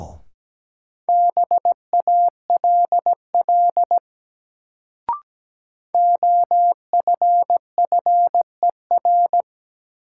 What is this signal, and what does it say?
Morse code, heard as a keyed tone, about 17 wpm, 700 Hz. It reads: BALL OFFER
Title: Sentences from Top 900 Words - New Word Drill - 17wpm